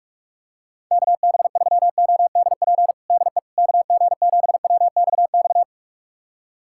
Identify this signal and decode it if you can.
Morse code, heard as a keyed tone, about 42 wpm, 700 Hz. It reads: KB3ODP DE KG7WXX
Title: USA Callsign DE Callsign 45wpm